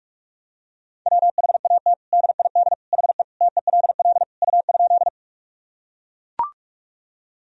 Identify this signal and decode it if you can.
Morse code, heard as a keyed tone, about 41 wpm, 700 Hz. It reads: WHAT DID HE TELL U?
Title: Sentences from Top 200 Words and Abbreviations 45wpm